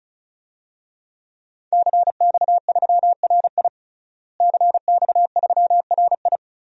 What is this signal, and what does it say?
Morse code, heard as a keyed tone, about 36 wpm, 700 Hz. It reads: CX3RI CX3RI
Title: DXCC-100 Call Signs - 5 Characters - Repeated 3x - 35wpm